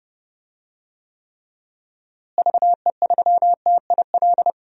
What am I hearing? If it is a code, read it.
Morse code, heard as a keyed tone, about 32 wpm, 700 Hz. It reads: VE3TIL
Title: DXCC-100 Call Signs - 6 Characters - Repeated 3x - 30wpm